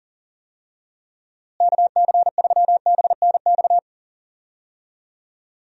Morse code, heard as a keyed tone, about 40 wpm, 700 Hz. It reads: KC3BNX
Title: USA Call Signs - 6 Characters - Repeated 2x - 40wpm